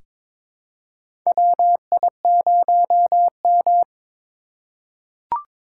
Morse code, heard as a keyed tone, about 22 wpm, 700 Hz. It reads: WI0M
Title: USA Call Signs - 4 Characters - 3x Character Spacing - 22wpm